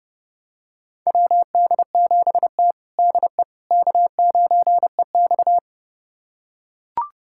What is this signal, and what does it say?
Morse code, heard as a keyed tone, about 32 wpm, 700 Hz. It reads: WD7T DE K9EX
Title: USA Callsign DE Callsign - 4 Characters - 30wpm